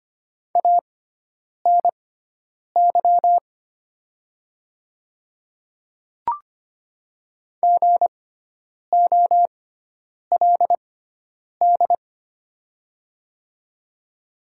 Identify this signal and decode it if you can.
Morse code, heard as a keyed tone, about 25 wpm, spaced as for 9 wpm, 700 Hz. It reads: ANY GOLD
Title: 6x Character Spacing - Top 500 Words - 25wpm